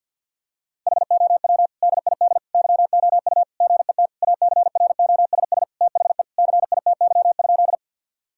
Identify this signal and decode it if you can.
Morse code, heard as a keyed tone, about 50 wpm, 700 Hz. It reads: HOW DID YOU GET ACROSS THE CITY?
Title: Sentences from Top 1000 Words - Review Part 5 - 50wpm